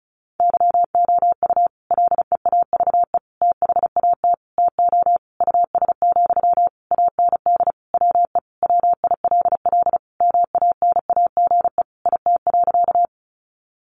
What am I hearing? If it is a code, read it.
Morse code, heard as a keyed tone, about 34 wpm, 700 Hz. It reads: YOU LEAVE THAT TO US, AND WE WILL MANAGE IT.
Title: Sentences from Top 1400 Words 35wpm